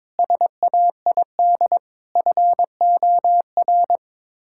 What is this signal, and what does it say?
Morse code, heard as a keyed tone, about 22 wpm, 700 Hz. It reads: SAID FOR
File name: practice / 22wpm / Sets of 8 Words - 22wpm